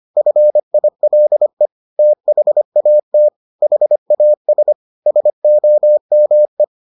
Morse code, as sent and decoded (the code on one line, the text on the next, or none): ..-. .. .-.. / - .... .- - / .... .- ... / ... --- -- .
FILE THAT HAS SOME